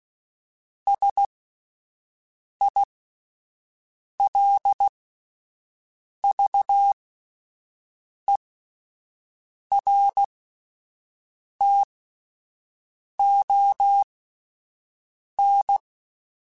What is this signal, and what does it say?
Morse code, heard as a keyed tone, about 16 wpm, spaced as for 5 wpm, 800 Hz. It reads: SILVERTON